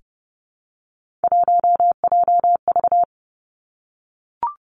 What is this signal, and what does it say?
Morse code, heard as a keyed tone, about 30 wpm, 700 Hz. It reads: 1JV
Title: Three Challenging Letter-Number Combinations 30wpm